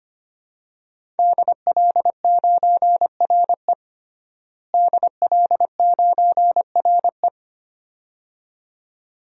Morse code, heard as a keyed tone, about 25 wpm, 700 Hz. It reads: DL9RE DL9RE
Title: DXCC-100 Call Signs - 5 Characters - Repeated 2x - 25wpm